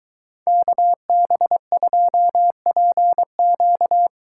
Morse code, heard as a keyed tone, about 22 wpm, 700 Hz. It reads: KB2PQ